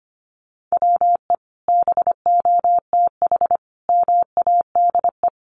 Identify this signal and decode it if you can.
Morse code, heard as a keyed tone, about 25 wpm, 700 Hz. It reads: WE BOTH MADE
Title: Sentences from Top 1500 Words 25wpm